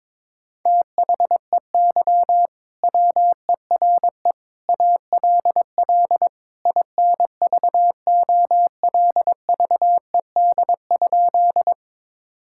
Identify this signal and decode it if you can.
Morse code, heard as a keyed tone, about 22 wpm, 700 Hz. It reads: THEY WERE ALL INVOLVED?
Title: Sentences from Top 1000 Words - Review Part 5 - 22wpm